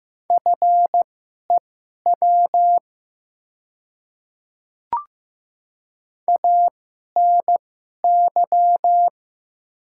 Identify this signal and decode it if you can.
Morse code, heard as a keyed tone, about 15 wpm, spaced as for 11 wpm, 700 Hz. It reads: FEW ANY
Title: 2x Character Spacing - Top 300 Words - 15wpm